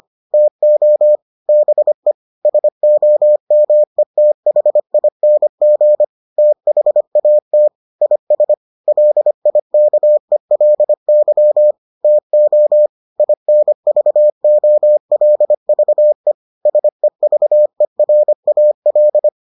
Morse code, heard as a keyed tone, about 25 wpm, 600 Hz. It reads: TO BE SOMETHING THAT IS LIKELY TO INVOLVE SEVERAL